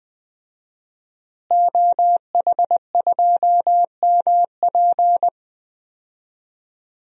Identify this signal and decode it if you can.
Morse code, heard as a keyed tone, about 21 wpm, 700 Hz. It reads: OH2MP
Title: DXCC-100 Call Signs - 5 Characters - Repeated 2x - 20wpm